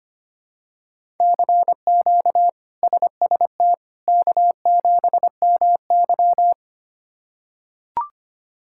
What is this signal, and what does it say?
Morse code, heard as a keyed tone, about 25 wpm, 700 Hz. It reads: CQ SST K7MY
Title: K1USN SST QSO Practice - 25wpm